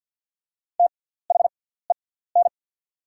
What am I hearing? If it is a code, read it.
Morse code, heard as a keyed tone, about 54 wpm, spaced as for 18 wpm, 690 Hz. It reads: THEN
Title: 6x Character Spacing - Top 200 Words - 50wpm